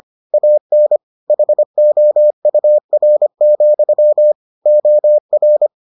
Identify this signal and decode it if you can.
Morse code, heard as a keyed tone, about 25 wpm, 600 Hz. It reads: AN HOUR, OR